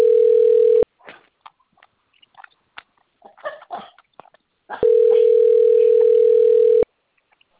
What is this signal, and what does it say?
Ringback tone (North America), 440 + 480 Hz, with speech.
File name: rg-1600-unknown- - - .51529